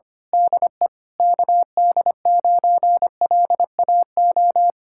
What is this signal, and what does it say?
Morse code, heard as a keyed tone, about 24 wpm, 700 Hz. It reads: DE KD9LAO
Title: Speed-Racing Common QSO Phrases - 25wpm